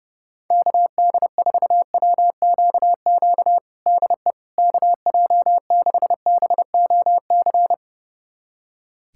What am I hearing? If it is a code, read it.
Morse code, heard as a keyed tone, about 30 wpm, 700 Hz. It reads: KD4WQQ DE KJ6BOC
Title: USA Callsign DE Callsign - 6 Characters - 30wpm